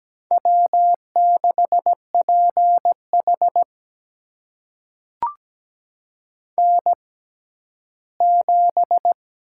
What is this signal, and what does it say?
Morse code, heard as a keyed tone, about 17 wpm, spaced as for 11 wpm, 700 Hz. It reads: W6PH N7
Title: USA Call Signs - 4 Characters - 6x Character Spacing - 17wpm